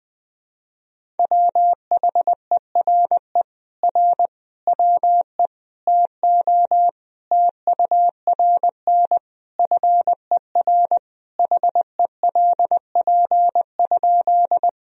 Morse code, heard as a keyed tone, about 20 wpm, 700 Hz. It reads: WHERE R WE TO TURN FER HELP?
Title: Sentences from Top 200 Words and Abbreviations 20wpm